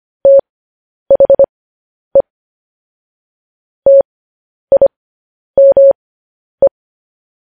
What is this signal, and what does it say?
Morse code, heard as a keyed tone, about 25 wpm, spaced as for 10 wpm, 550 Hz. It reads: THE TIME
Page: news headlines in morse code at 10 wpm